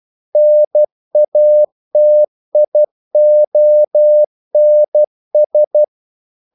Morse code, heard as a keyed tone, about 12 wpm, 600 Hz. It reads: NATIONS
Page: English quotations in CW (telegraphy - morse)